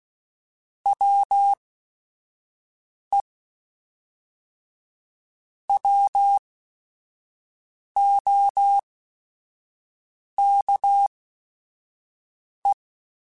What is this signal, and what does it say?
Morse code, heard as a keyed tone, about 16 wpm, spaced as for 5 wpm, 800 Hz. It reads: WE WOKE
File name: Weather Sets / Weather Sets-mp3s / Weather 3 17 @5